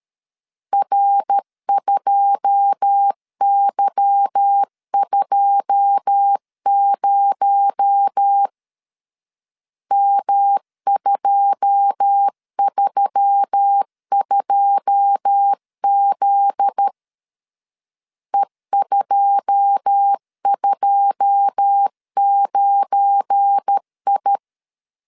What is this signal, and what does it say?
Morse code, heard as a keyed tone, about 13 wpm, 800 Hz. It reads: R2Y20 M232Z E229I
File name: LEZIONI / 9 2 12wpm 2301